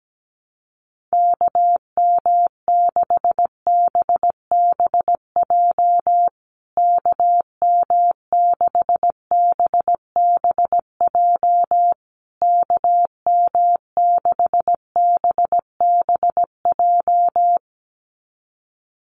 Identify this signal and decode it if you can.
Morse code, heard as a keyed tone, about 17 wpm, 700 Hz. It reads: KM6BBJ KM6BBJ KM6BBJ
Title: USA Call Signs Repeated 3x 17wpm